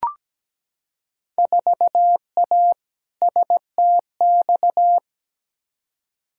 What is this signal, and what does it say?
Morse code, heard as a keyed tone, about 17 wpm, 700 Hz. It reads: 4A STX
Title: Field Day Exchanges Encore - 17wpm